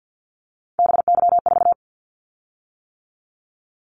Morse code, heard as a keyed tone, about 48 wpm, 700 Hz. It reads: BY4
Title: Callsign Prefixes for Top 100 DXCC - 50wpm